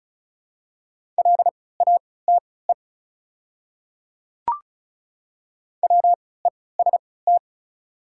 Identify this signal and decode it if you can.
Morse code, heard as a keyed tone, about 34 wpm, spaced as for 20 wpm, 690 Hz. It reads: LATE WEST
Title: 3x Character Spacing - Top 500 Words - 35wpm